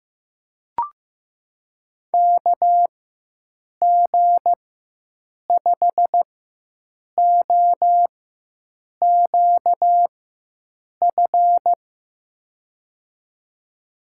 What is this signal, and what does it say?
Morse code, heard as a keyed tone, about 15 wpm, spaced as for 7 wpm, 700 Hz. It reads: KG5OQF
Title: USA Call Signs - 6 Characters - 4x Character Spacing - 15wpm